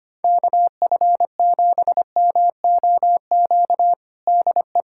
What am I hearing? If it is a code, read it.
Morse code, heard as a keyed tone, about 25 wpm, 700 Hz. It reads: KF7MOQ DE
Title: USA Callsign DE Callsign - 6 Characters - 25wpm